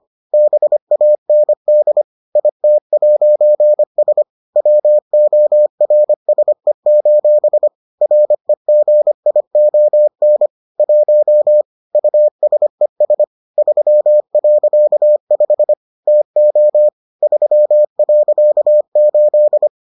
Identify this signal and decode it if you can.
Morse code, heard as a keyed tone, about 25 wpm, 600 Hz. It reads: BAND IT'S WORSE: REGION 1 USES 3.5 TO 3.8